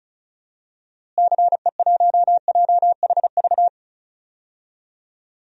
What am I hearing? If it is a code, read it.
Morse code, heard as a keyed tone, about 34 wpm, 700 Hz. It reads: CE1JHV
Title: DXCC-100 Call Signs - 6 Characters - Repeated 3x - 35wpm